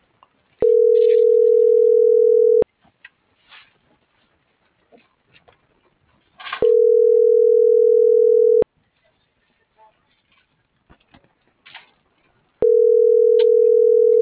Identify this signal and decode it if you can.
Ringback tone (North America), 440 + 480 Hz.